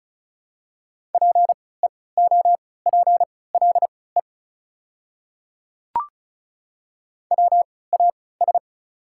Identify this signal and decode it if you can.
Morse code, heard as a keyed tone, about 34 wpm, spaced as for 20 wpm, 690 Hz. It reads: PEOPLE WAS
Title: 3x Character Spacing - Top 200 Words - 35wpm